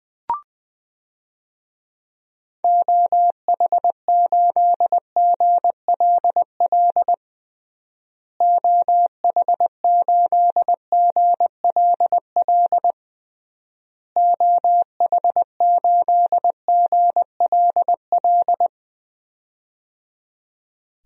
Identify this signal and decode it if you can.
Morse code, heard as a keyed tone, about 21 wpm, 700 Hz. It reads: OH8GLL OH8GLL OH8GLL
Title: DXCC-100 Call Signs - 6 Characters - Repeated 3x - 20wpm